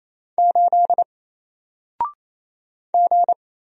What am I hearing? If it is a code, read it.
Morse code, heard as a keyed tone, about 28 wpm, 700 Hz. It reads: 8 G
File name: Single-Letter-Number-28wpm